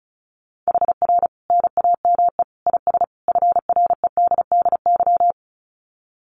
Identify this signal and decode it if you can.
Morse code, heard as a keyed tone, about 34 wpm, 700 Hz. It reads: HR NAME IS FREDDY
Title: Common QSO Phrases 35wpm